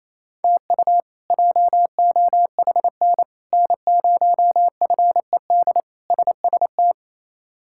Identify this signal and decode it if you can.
Morse code, heard as a keyed tone, about 28 wpm, 700 Hz. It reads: TU JOHN N0FED SST
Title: K1USN SST QSO Practice - 28wpm